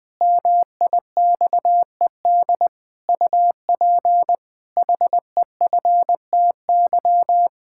Morse code, heard as a keyed tone, about 20 wpm, 700 Hz. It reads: MIXED UP HEFTY
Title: Pangram Challenge v2 - 20wpm